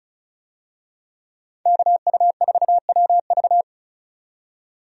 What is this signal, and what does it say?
Morse code, heard as a keyed tone, about 36 wpm, 700 Hz. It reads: KU4WV
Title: USA Call Signs - 5 Characters - 35wpm